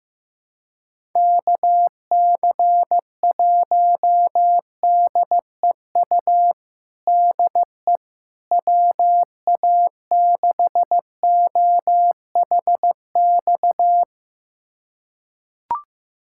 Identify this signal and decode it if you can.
Morse code, heard as a keyed tone, about 15 wpm, 700 Hz. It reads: KC1DEU DE WA6OHX
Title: USA Callsign DE Callsign - 6 Characters - 15wpm